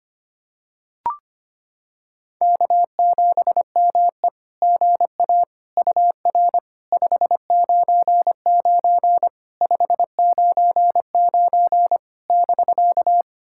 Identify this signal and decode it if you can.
Morse code, heard as a keyed tone, about 25 wpm, 700 Hz. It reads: K7ME GA UR 599 599 <BK>
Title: SOTA QSO Practice 25wpm